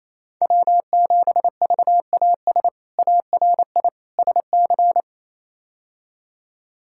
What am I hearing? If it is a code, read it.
Morse code, heard as a keyed tone, about 28 wpm, 700 Hz. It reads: W7VAS ARI SC
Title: K1USN SST QSO Practice - 28wpm